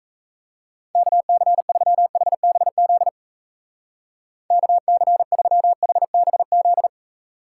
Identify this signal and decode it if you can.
Morse code, heard as a keyed tone, about 34 wpm, 700 Hz. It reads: KC3HBZ KC3HBZ
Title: Speed Racing - USA Call Signs 28wpm